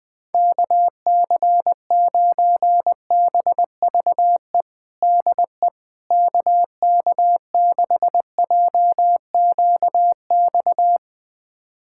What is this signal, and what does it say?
Morse code, heard as a keyed tone, about 20 wpm, 700 Hz. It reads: KC9BVE DE KK6JQX
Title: USA Callsign DE Callsign - 6 Characters - 20wpm